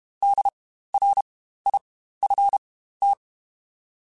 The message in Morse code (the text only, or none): DRIFT